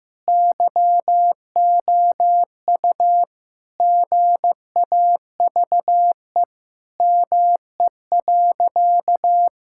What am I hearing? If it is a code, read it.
Morse code, heard as a keyed tone, about 15 wpm, 700 Hz. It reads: YOU GAVE ME.
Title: Sentences from Top 1000 Words - Review Part 6 - 15wpm